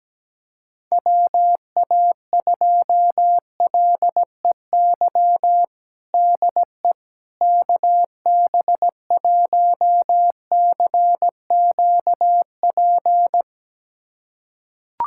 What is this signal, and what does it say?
Morse code, heard as a keyed tone, about 17 wpm, 700 Hz. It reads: WA2LEY DE KB1CQP
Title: USA Callsign DE Callsign 17wpm